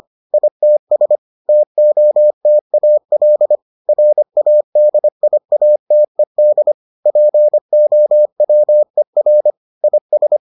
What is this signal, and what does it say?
Morse code, heard as a keyed tone, about 25 wpm, 600 Hz. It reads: ITS TOTAL RADIATED POWER IS